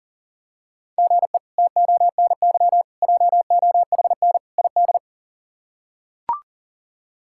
Morse code, heard as a keyed tone, about 40 wpm, 700 Hz. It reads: GE TONY JOHN ID
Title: K1USN SST QSO Practice - 40wpm